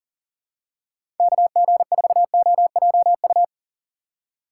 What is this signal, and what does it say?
Morse code, heard as a keyed tone, about 40 wpm, 700 Hz. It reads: KG4OJU
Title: USA Call Signs - 6 Characters - 4x Character Spacing - 40wpm